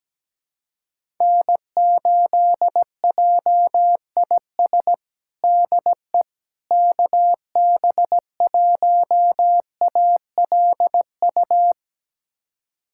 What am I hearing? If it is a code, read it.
Morse code, heard as a keyed tone, about 17 wpm, 700 Hz. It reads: N8JIS DE KB1ALU
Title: USA Callsign DE Callsign 17wpm